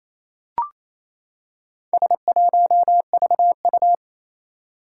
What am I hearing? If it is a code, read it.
Morse code, heard as a keyed tone, about 27 wpm, 700 Hz. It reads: S1VU
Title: Four Challenging Letter-Number Combinations 28wpm